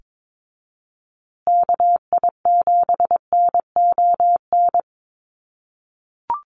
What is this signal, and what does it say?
Morse code, heard as a keyed tone, about 22 wpm, 700 Hz. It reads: KI7NON
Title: USA Call Signs Repeated 3x 22wpm